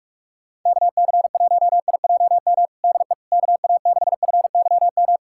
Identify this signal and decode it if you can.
Morse code, heard as a keyed tone, about 46 wpm, 700 Hz. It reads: KC1IJM DE KA6FYM
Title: USA Callsign DE Callsign - 6 Characters - 45wpm